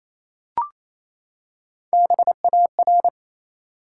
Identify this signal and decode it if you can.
Morse code, heard as a keyed tone, about 28 wpm, 700 Hz. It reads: BAR
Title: Three Letter Words - 28wpm